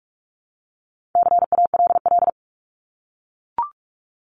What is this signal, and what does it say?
Morse code, heard as a keyed tone, about 44 wpm, 700 Hz. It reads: CALL
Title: Top 300 Words 45wpm